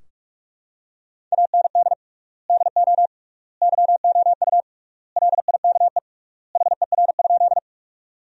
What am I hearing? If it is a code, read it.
Morse code, heard as a keyed tone, about 44 wpm, 700 Hz. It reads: AND DO YOU LIKE HER?
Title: 3x Word-Spacing - Sentences from Top 100 Words - 45wpm